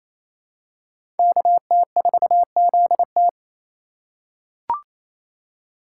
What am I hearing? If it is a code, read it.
Morse code, heard as a keyed tone, about 28 wpm, 700 Hz. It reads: KT4ZT
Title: USA Call Signs - 5 Characters - 8x Character Spacing - 28wpm